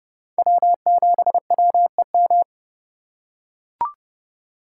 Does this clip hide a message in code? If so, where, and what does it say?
Yes: Morse code from 0.4 to 2.4 s, heard as a keyed tone, about 29 wpm, 700 Hz. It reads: W7WEM